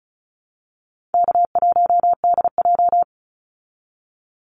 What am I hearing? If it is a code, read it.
Morse code, heard as a keyed tone, about 34 wpm, 700 Hz. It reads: K1DJ
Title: Call Signs Repeated 3x - 35wpm